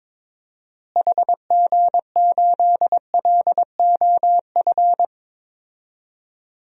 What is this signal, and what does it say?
Morse code, heard as a keyed tone, about 22 wpm, 700 Hz. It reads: HG8LOF
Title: DXCC-100 Call Signs - 6 Characters - Repeated 2x - 22wpm